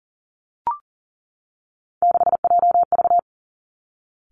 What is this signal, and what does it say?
Morse code, heard as a keyed tone, about 38 wpm, 700 Hz. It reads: 6JV
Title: Three Challenging Letter-Number Combinations 40wpm